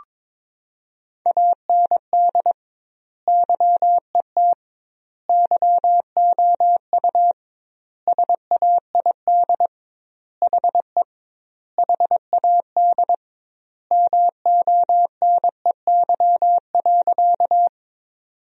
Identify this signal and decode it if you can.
Morse code, heard as a keyed tone, about 21 wpm, 700 Hz. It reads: AND YET YOU SAID HE HAD MONEY.
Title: 2x Word-Spacing - Sentences from Top 500 Words - 22wpm